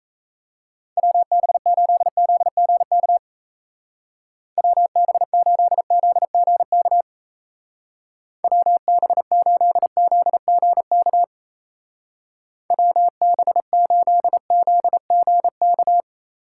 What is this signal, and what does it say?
Morse code, heard as a keyed tone, about 34 wpm, 700 Hz. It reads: WB8ZGK WB8ZGK WB8ZGK WB8ZGK
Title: Speed Racing - USA 2x3 Call Signs 28wpm